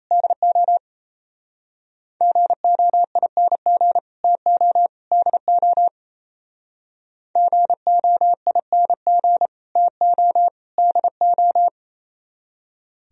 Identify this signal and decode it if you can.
Morse code, heard as a keyed tone, about 32 wpm, 700 Hz. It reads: DO GOING TO DO GOING TO DO
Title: Speed-Racing - Sets of 3 Words 28wpm